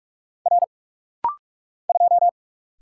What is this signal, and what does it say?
Morse code, heard as a keyed tone, about 46 wpm, 700 Hz. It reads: R 2
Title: Single Letter-Number Alt - Rapid-Fire 45wpm